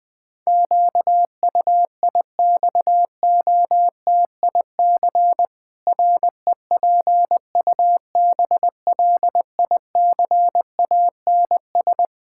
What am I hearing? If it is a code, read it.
Morse code, heard as a keyed tone, about 20 wpm, 700 Hz. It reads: QUIXOTIC REPUBLICANS
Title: Pangram Challenge v2 - 20wpm